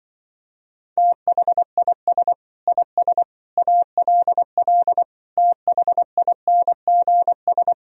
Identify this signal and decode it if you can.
Morse code, heard as a keyed tone, about 24 wpm, 700 Hz. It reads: THIS IS ALL THINGS
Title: Speed-Racing - Sets of 5 Words 20wpm